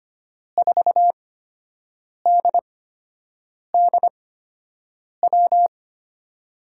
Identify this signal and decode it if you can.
Morse code, heard as a keyed tone, about 25 wpm, spaced as for 7 wpm, 700 Hz. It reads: 4DDW
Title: USA Call Signs - 6 Characters - 8x Character Spacing - 25wpm